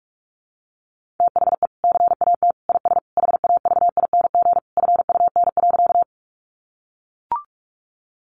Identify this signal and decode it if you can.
Morse code, heard as a keyed tone, about 44 wpm, 700 Hz. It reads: THE CAT IS HAVING FUN.
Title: Easy Phrases 45wpm